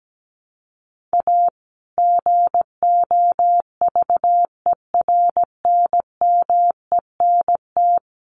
Morse code, heard as a keyed tone, about 17 wpm, 700 Hz. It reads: A GOVERNMENT
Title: Sentences from Top 1400 Words 17wpm